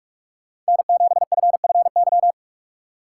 Morse code, heard as a keyed tone, about 43 wpm, 700 Hz. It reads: N7FFY
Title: USA Call Signs - 2x Character Spacing - 45wpm